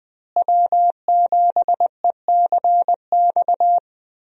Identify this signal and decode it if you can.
Morse code, heard as a keyed tone, about 21 wpm, 700 Hz. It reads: W7ECX
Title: USA Call Signs - 5 Characters - Repeated 2x - 20wpm